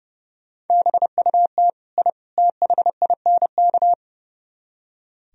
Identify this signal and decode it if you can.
Morse code, heard as a keyed tone, about 39 wpm, 700 Hz. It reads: BUT I THINK